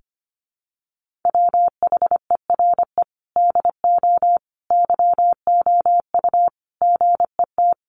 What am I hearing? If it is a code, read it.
Morse code, heard as a keyed tone, about 25 wpm, 700 Hz. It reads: WHERE DO YOU GET
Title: Sentences from Top 1400 Words 25wpm